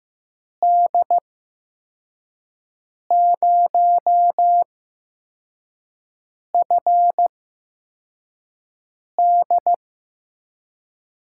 Morse code, heard as a keyed tone, about 16 wpm, spaced as for 4 wpm, 700 Hz. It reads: D0FD
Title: USA Call Signs - 5 Characters - 8x Character Spacing - 15wpm